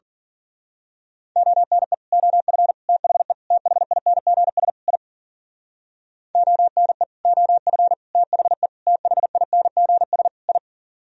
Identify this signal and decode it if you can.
Morse code, heard as a keyed tone, about 43 wpm, 700 Hz. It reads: ONE OF THE THINGS I ONE OF THE THINGS I
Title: Speed-Racing - Sets of 5 Words 40wpm